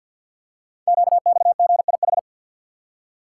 Morse code, heard as a keyed tone, about 50 wpm, 700 Hz. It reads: QXGIH
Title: Code Groups - 5 Letters - 50wpm